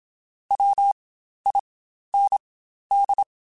Morse code, heard as a keyed tone, about 27 wpm, spaced as for 12 wpm, 800 Hz. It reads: WIND
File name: Spaced Characters / Spaced Characters-mp3s / Weather 5 22 sp @25